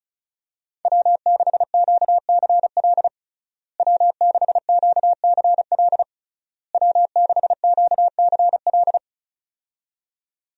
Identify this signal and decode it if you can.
Morse code, heard as a keyed tone, about 36 wpm, 700 Hz. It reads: W6QCL W6QCL W6QCL
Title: USA Call Signs - 5 Characters - Repeated 3x - 35wpm